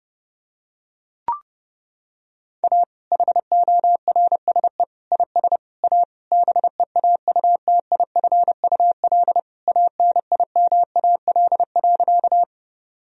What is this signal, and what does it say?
Morse code, heard as a keyed tone, about 30 wpm, 700 Hz. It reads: A HORSE IS A BEAUTIFUL ANIMAL.
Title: Sentences from Top 1000 Words - Review Part 1 - 30wpm